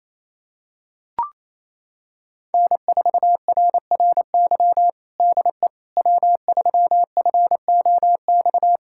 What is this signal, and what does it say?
Morse code, heard as a keyed tone, about 28 wpm, 700 Hz. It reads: N4RRY DE W3FOX
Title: USA Callsign DE Callsign - 5 Characters - 28wpm